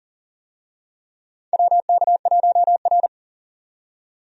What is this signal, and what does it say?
Morse code, heard as a keyed tone, about 40 wpm, 700 Hz. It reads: WK1R
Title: USA Call Signs - 4 Characters - 40wpm